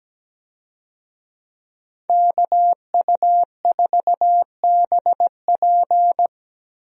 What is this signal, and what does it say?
Morse code, heard as a keyed tone, about 17 wpm, 700 Hz. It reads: KU4BP
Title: USA Call Signs - 5 Characters - Repeated 2x - 17wpm